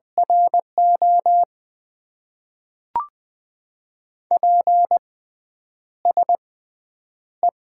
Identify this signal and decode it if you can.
Morse code, heard as a keyed tone, about 20 wpm, spaced as for 9 wpm, 700 Hz. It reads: RO PSE
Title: Top 90 QSO Elements - 6x Character Spacing - 20wpm